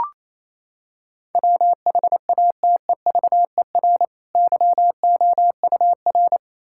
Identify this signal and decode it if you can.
Morse code, heard as a keyed tone, about 28 wpm, 700 Hz. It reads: WHATEVER YOUR
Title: Sentences from Top 1600 Words 28wpm